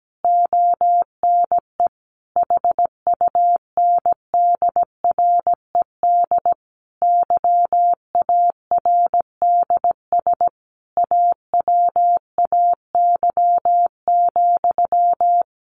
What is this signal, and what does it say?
Morse code, heard as a keyed tone, about 17 wpm, 700 Hz. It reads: ONE HUNDRED YARDS AWAY,
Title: A Sound Of Thunder 17wpm